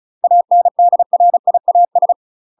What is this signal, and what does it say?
Morse code, heard as a keyed tone, about 36 wpm, 700 Hz. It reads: ANDRIAS
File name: UK MASTER QSOs TRAINING / 35WPM / UK MASTER QSOs 0011